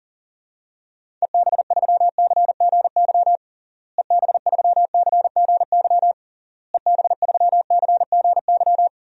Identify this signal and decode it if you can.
Morse code, heard as a keyed tone, about 40 wpm, 700 Hz. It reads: EB3CGY EB3CGY EB3CGY
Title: DXCC-100 Call Signs - 6 Characters - Repeated 3x - 40wpm